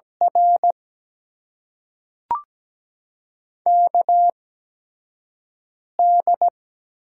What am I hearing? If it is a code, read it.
Morse code, heard as a keyed tone, about 17 wpm, spaced as for 5 wpm, 700 Hz. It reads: R KD
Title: USA Call Signs - 6 Characters - 8x Character Spacing - 17wpm